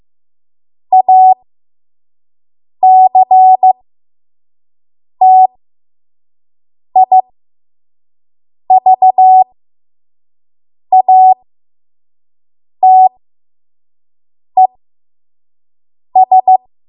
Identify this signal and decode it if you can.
Morse code, heard as a keyed tone, about 15 wpm, spaced as for 5 wpm, 750 Hz. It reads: ACTIVATES
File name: Morse / 05WPM